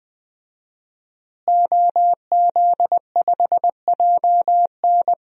Morse code, heard as a keyed tone, about 20 wpm, 700 Hz. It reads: OZ5JN
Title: DXCC-100 Call Signs - 5 Characters - Repeated 3x - 20wpm